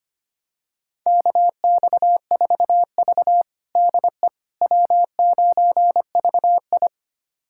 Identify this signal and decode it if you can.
Morse code, heard as a keyed tone, about 25 wpm, 700 Hz. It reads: KX4V DE W9VI
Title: USA Callsign DE Callsign - 4 Characters - 25wpm